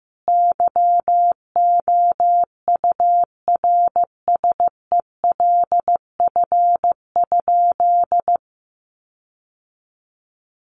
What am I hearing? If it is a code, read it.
Morse code, heard as a keyed tone, about 14 wpm, 700 Hz. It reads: YOURSELF?
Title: Sentences from Top 1300 Words 15wpm